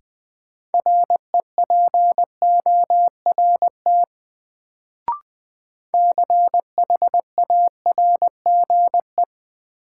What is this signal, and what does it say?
Morse code, heard as a keyed tone, about 20 wpm, 700 Hz. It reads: REPORT CHARGE
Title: Sentences from Top 900 Words - New Word Drill - 20wpm